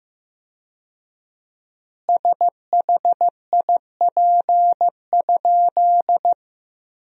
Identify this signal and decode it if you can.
Morse code, heard as a keyed tone, about 15 wpm, 700 Hz. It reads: SHIP?
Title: 4x Word-Spacing - Sentences from Top 400 Words - 15wpm